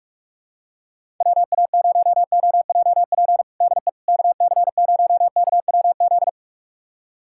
Morse code, heard as a keyed tone, about 42 wpm, 700 Hz. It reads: WA0OJP DE KC0KWZ
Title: USA Callsign DE Callsign - 6 Characters - 45wpm